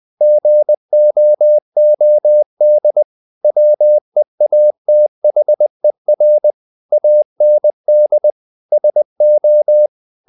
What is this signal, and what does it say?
Morse code, heard as a keyed tone, about 20 wpm, 600 Hz. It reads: GOOD WEATHER AND SO
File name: the old man and the sea / the old man and the sea part 1 20wpm